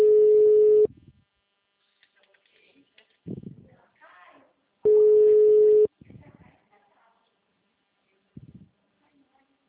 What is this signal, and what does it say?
Ringback tone (Europe, 425 Hz), with speech.